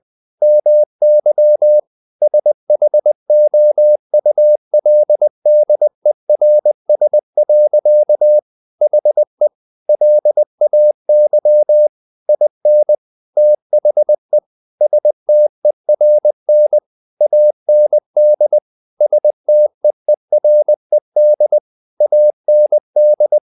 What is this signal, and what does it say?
Morse code, heard as a keyed tone, about 20 wpm, 600 Hz. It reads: MY SHOULDERS. HE LAY IN THE STERN AND STEERED AND
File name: the old man and the sea part 2 20wpm